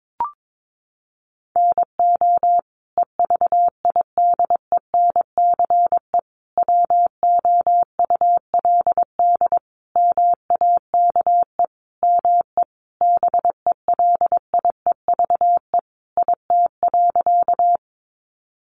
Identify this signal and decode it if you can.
Morse code, heard as a keyed tone, about 22 wpm, 700 Hz. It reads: NO EVIDENCE WOULD MAKE ME BELIEVE IT.
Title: Sentences from Top 1300 Words 22wpm